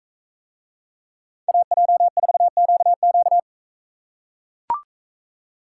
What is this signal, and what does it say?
Morse code, heard as a keyed tone, about 43 wpm, 700 Hz. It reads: AJ4QQ